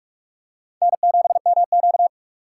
Morse code, heard as a keyed tone, about 46 wpm, 700 Hz. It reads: N7MQ
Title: USA Call Signs - 2x Character Spacing - 45wpm